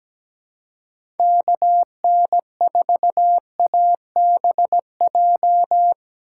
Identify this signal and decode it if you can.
Morse code, heard as a keyed tone, about 17 wpm, 700 Hz. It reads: KN4ABJ
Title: USA Call Signs - 6 Characters - 6x Character Spacing - 17wpm